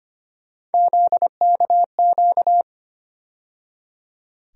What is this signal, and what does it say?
Morse code, heard as a keyed tone, about 25 wpm, 700 Hz. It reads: ZKQ
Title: USA Call Sign Postfixes - 3 Letters 25wpm